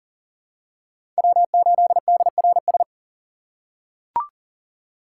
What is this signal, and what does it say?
Morse code, heard as a keyed tone, about 40 wpm, 700 Hz. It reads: W8DRS
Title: USA Call Signs - 5 Characters - 3x Character Spacing - 40wpm